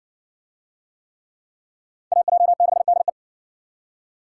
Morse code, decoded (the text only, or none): AJ6DE